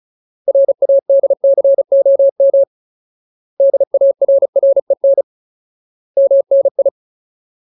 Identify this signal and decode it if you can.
Morse code, heard as a keyed tone, about 34 wpm, 550 Hz. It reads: RADCOM DARREN MNI
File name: Ultimate Word List / 35WPM Ultimate